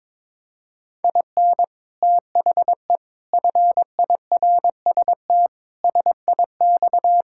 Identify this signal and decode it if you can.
Morse code, heard as a keyed tone, about 22 wpm, 700 Hz. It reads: IN THE FIRST SIX
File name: practice / 22wpm / Sets of 8 Words - 22wpm